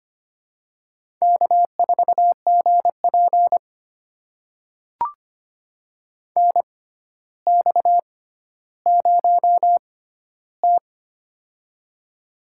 Morse code, heard as a keyed tone, about 25 wpm, spaced as for 13 wpm, 700 Hz. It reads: K4GP NX0T
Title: USA Call Signs - 4 Characters - 6x Character Spacing - 25wpm